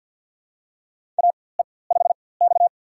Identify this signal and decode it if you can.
Morse code, heard as a keyed tone, about 53 wpm, spaced as for 24 wpm, 700 Hz. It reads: AE5X
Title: USA Call Signs - 4x Character Spacing - 50wpm